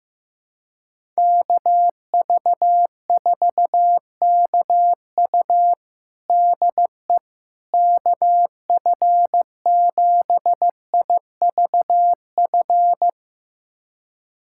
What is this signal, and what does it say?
Morse code, heard as a keyed tone, about 15 wpm, 700 Hz. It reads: KV4KU DE KF7IVF
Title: USA Callsign DE Callsign 15wpm